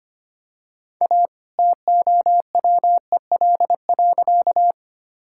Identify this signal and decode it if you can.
Morse code, heard as a keyed tone, about 25 wpm, 700 Hz. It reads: A TOWEL.
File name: practice / 25wpm / All-Age Appropriate Jokes - 25wpm